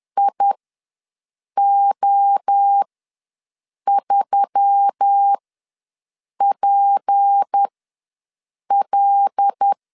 Morse code, heard as a keyed tone, about 10 wpm, spaced as for 6 wpm, 800 Hz. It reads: IO3PL
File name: LEZIONI / 6 PKX0 / 6 P ws 10wpm 3751